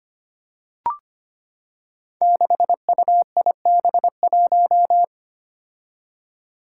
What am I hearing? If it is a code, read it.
Morse code, heard as a keyed tone, about 25 wpm, 700 Hz. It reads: 6UIB1
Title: Five Challenging Letter-Number Combinations 25wpm